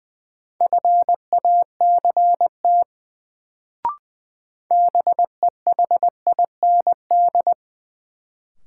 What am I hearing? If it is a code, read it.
Morse code, heard as a keyed tone, about 20 wpm, 700 Hz. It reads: FACT BEHIND
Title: Sentences from Top 500 Words - New Word Drill - 20wpm